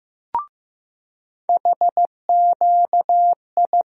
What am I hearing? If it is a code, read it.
Morse code, heard as a keyed tone, about 15 wpm, 700 Hz. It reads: HQI
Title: USA Call Sign Postfixes - 3 Letters 15wpm